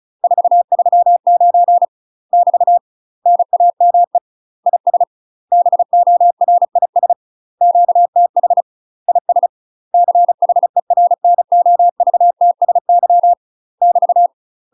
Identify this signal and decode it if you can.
Morse code, heard as a keyed tone, about 36 wpm, 700 Hz. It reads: 439 <BT> NAME IS BORIS QTH IS CHERNOVTSY <BT>
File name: UK MASTER QSOs TRAINING/35WPM